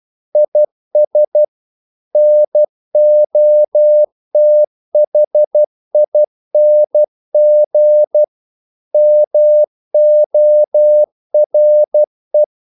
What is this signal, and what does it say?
Morse code, heard as a keyed tone, about 12 wpm, 600 Hz. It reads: IS NOTHING MORE
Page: English quotations in CW (telegraphy - morse)